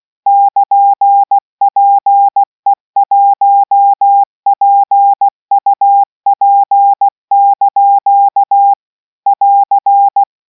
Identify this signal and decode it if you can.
Morse code, heard as a keyed tone, about 16 wpm, 800 Hz. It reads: <KN>PE1PUP) <AR>